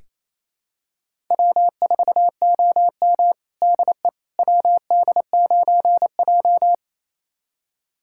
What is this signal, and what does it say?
Morse code, heard as a keyed tone, about 28 wpm, 700 Hz. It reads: W4OM DE WD9J
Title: USA Callsign DE Callsign - 4 Characters - 28wpm